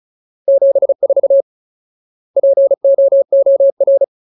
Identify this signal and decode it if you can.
Morse code, heard as a keyed tone, about 36 wpm, 550 Hz. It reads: 74 POOR
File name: Ultimate Word List / 35WPM Ultimate